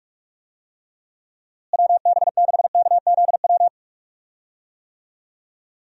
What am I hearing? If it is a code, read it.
Morse code, heard as a keyed tone, about 47 wpm, 700 Hz. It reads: WB6KZW
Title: USA Call Signs - 6 Characters - Repeated 3x - 45wpm